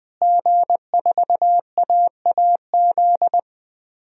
Morse code, heard as a keyed tone, about 20 wpm, 700 Hz. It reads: G4AAZ
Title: DXCC-100 Call Signs - 5 Characters - Repeated 2x - 20wpm